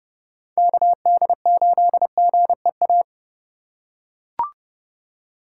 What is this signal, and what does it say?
Morse code, heard as a keyed tone, about 28 wpm, 700 Hz. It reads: KD8GEA